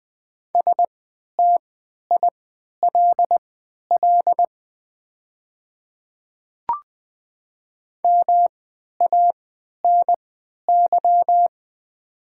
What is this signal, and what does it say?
Morse code, heard as a keyed tone, about 20 wpm, spaced as for 11 wpm, 700 Hz. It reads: STILL MANY
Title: 3x Character Spacing - Top 300 Words - 20wpm